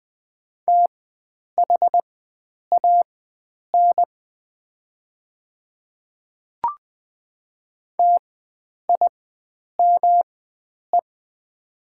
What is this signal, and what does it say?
Morse code, heard as a keyed tone, about 20 wpm, spaced as for 9 wpm, 700 Hz. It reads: THAN TIME
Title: 4x Character Spacing - Top 100 Words - 20wpm